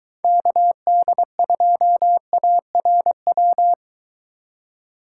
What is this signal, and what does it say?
Morse code, heard as a keyed tone, about 23 wpm, 700 Hz. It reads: KD2ARW